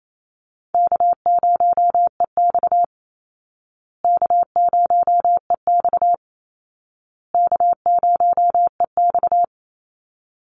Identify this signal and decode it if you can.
Morse code, heard as a keyed tone, about 28 wpm, 700 Hz. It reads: K0EX K0EX K0EX
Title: Call Signs Repeated 2x - 28wpm